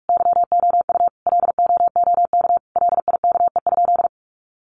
Morse code, heard as a keyed tone, about 44 wpm, 700 Hz. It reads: YOU LOOK LIKE?
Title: Sentences from Top 100 Words 45wpm